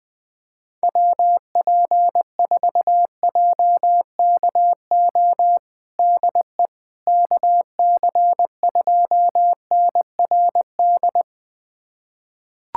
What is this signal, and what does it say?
Morse code, heard as a keyed tone, about 20 wpm, 700 Hz. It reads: WP4JKO DE KC2NRD
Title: USA Callsign DE Callsign 20wpm